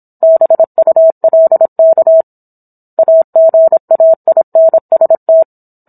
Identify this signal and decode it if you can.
Morse code, heard as a keyed tone, about 27 wpm, 650 Hz. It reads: BULK AGAINST